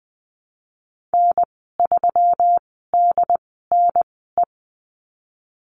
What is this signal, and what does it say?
Morse code, heard as a keyed tone, about 20 wpm, spaced as for 15 wpm, 700 Hz. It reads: N3DNE
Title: 161 Lesson 26g - Letter g - Call Signs 20wpm